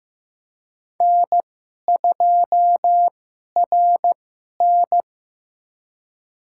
Morse code, heard as a keyed tone, about 16 wpm, spaced as for 11 wpm, 700 Hz. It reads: N2RN
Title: USA Call Signs - 2x Character Spacing - 15wpm